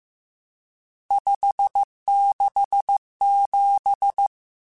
Morse code, heard as a keyed tone, about 15 wpm, 800 Hz. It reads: 567
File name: RSTs / RST-mp3s / Various RST @13